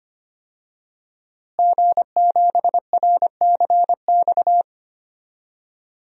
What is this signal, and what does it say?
Morse code, heard as a keyed tone, about 25 wpm, 700 Hz. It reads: G7RCX
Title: DXCC-100 Call Signs - 5 Characters - Repeated 3x - 25wpm